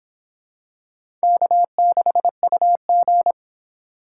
Morse code, heard as a keyed tone, about 26 wpm, 700 Hz. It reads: K6UG